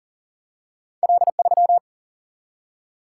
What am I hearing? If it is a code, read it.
Morse code, heard as a keyed tone, about 40 wpm, 700 Hz. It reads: L3
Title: Single-Letter Single-Number 40wpm